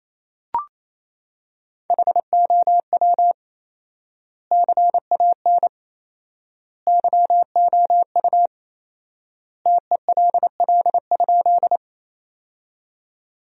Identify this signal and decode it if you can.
Morse code, heard as a keyed tone, about 28 wpm, 700 Hz. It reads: HOW CAN YOU TELL?
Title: 4x Word-Spacing - Sentences from Top 200 Words - 28wpm